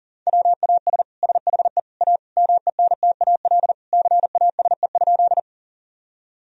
Morse code, heard as a keyed tone, about 40 wpm, 700 Hz. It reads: WAS SHE A MENTAL CASE?
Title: Sentences from Top 1000 Words - Review Part 4 - 40wpm